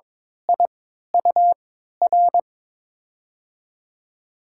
Morse code, heard as a keyed tone, about 21 wpm, spaced as for 12 wpm, 700 Hz. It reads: IUR
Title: USA Call Signs - 6 Characters - 3x Character Spacing - 22wpm